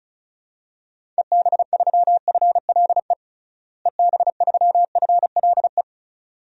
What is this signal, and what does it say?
Morse code, heard as a keyed tone, about 36 wpm, 700 Hz. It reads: EB3FLE EB3FLE
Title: DXCC-100 Call Signs - 6 Characters - Repeated 2x - 35wpm